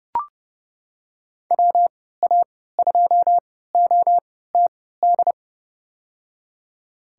Morse code, heard as a keyed tone, about 30 wpm, spaced as for 17 wpm, 700 Hz. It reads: WA2OTD
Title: USA Call Signs - 6 Characters - 3x Character Spacing - 30wpm